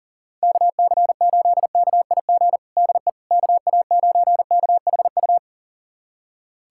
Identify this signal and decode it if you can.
Morse code, heard as a keyed tone, about 40 wpm, 700 Hz. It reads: KC8KIG DE KA9KHU
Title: USA Callsign DE Callsign - 6 Characters - 40wpm